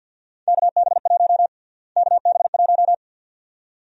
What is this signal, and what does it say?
Morse code, heard as a keyed tone, about 51 wpm, 700 Hz. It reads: KB1 KB1
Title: USA Call Sign Prefixes - Repeated 2x - 50wpm